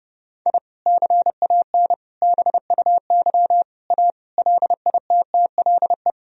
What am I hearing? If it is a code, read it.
Morse code, heard as a keyed tone, about 32 wpm, 700 Hz. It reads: I CAN BUY A LITTLE
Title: Sentences from Top 1000 Words - Review Part 1 - 30wpm